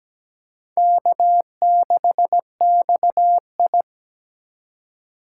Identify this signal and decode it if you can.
Morse code, heard as a keyed tone, about 20 wpm, 700 Hz. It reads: K6XI
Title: Speed Racing - USA Call Signs 17wpm